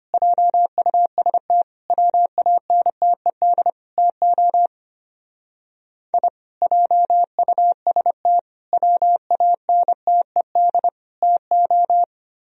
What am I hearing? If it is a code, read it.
Morse code, heard as a keyed tone, about 33 wpm, 700 Hz. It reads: JUST WANTED TO I JUST WANTED TO